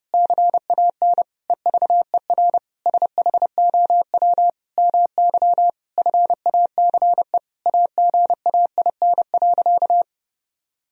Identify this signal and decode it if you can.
Morse code, heard as a keyed tone, about 30 wpm, 700 Hz. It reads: CAN EVER SHOW MY FACE AGAIN.
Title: Sentences from Top 1000 Words - Review Part 2 - 30wpm